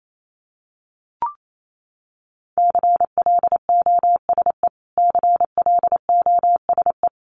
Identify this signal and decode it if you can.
Morse code, heard as a keyed tone, about 28 wpm, 700 Hz. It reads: CLOSE CLOSE
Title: Top 500 Words Repeated 28wpm